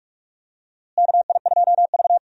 Morse code, heard as a keyed tone, about 46 wpm, 700 Hz. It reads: KI2V
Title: USA Call Signs - 4 Characters - Repeated 3x - 45wpm